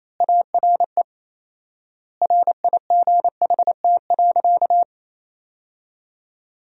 Morse code, heard as a keyed tone, about 28 wpm, 700 Hz. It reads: ARE RIGHT.
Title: 4x Word-Spacing - Sentences from Top 300 Words - 28wpm